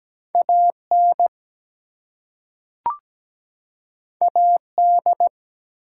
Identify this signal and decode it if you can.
Morse code, heard as a keyed tone, about 17 wpm, 700 Hz. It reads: AN AD
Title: Two Letter Words - 17wpm